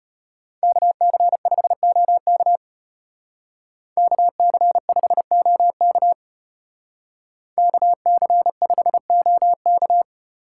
Morse code, heard as a keyed tone, about 30 wpm, 700 Hz. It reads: KC5OK KC5OK KC5OK
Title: Speed Racing - USA 2x2 Call Signs 25wpm